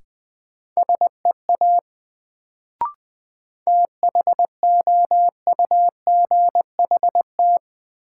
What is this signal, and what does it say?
Morse code, heard as a keyed tone, about 20 wpm, 700 Hz. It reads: SEA THOUGHT
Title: Sentences from Top 300 Words - New Word Drill - 20wpm